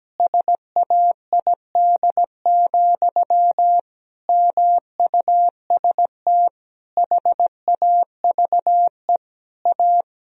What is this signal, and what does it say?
Morse code, heard as a keyed tone, about 17 wpm, 700 Hz. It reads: SAID, MUST HAVE A
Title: Sentences from Top 1000 Words - Review Part 1 - 17wpm